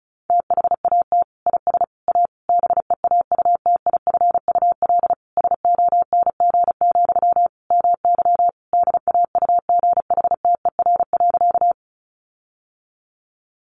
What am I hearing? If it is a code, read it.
Morse code, heard as a keyed tone, about 34 wpm, 700 Hz. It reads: THAT IS A BEAUTIFUL SONG, MY DAUGHTER.
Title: Sentences from Top 1500 Words 35wpm